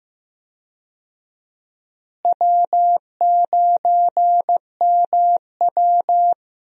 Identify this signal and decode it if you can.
Morse code, heard as a keyed tone, about 15 wpm, 700 Hz. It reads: W9MW
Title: USA Call Signs - 4 Characters - Repeated 2x - 15wpm